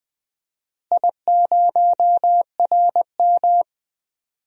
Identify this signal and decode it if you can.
Morse code, heard as a keyed tone, about 20 wpm, 700 Hz. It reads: I0RM
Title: DXCC-100 Call Signs - 4 Characters - Repeated 2x - 20wpm